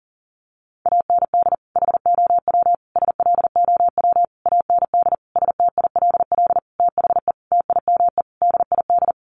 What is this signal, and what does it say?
Morse code, heard as a keyed tone, about 40 wpm, 700 Hz. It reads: AND HOW SLOW AND STILL THE TIME DID
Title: Sentences from Top 1500 Words 40wpm